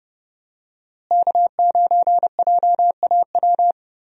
Morse code, heard as a keyed tone, about 32 wpm, 700 Hz. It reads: K9JAW
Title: USA Call Signs - 5 Characters - Repeated 3x - 30wpm